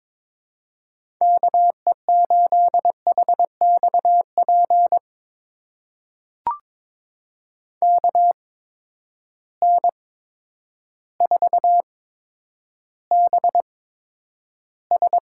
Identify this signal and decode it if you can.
Morse code, heard as a keyed tone, about 22 wpm, spaced as for 10 wpm, 700 Hz. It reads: KE8HXP KN4BS
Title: USA Call Signs - 6 Characters - 8x Character Spacing - 22wpm